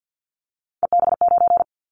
Morse code, heard as a keyed tone, about 50 wpm, 700 Hz. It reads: EB9
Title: Callsign Prefixes for Top 100 DXCC - 50wpm